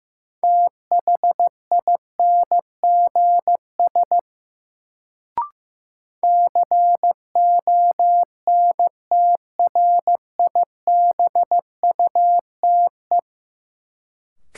Morse code, heard as a keyed tone, about 15 wpm, 700 Hz. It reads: THINGS CONTRIBUTE
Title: Sentences from Top 1400 Words - New Word Drill Alt - 15wpm